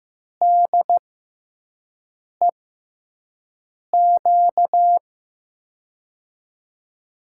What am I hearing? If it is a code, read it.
Morse code, heard as a keyed tone, about 15 wpm, spaced as for 5 wpm, 700 Hz. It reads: DEQ
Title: USA Call Signs - 5 Characters - 6x Character Spacing - 15wpm